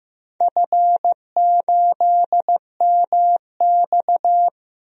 Morse code, heard as a keyed tone, about 15 wpm, 700 Hz. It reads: F8MX
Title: DXCC-100 Call Signs - 4 Characters - Repeated 3x - 15wpm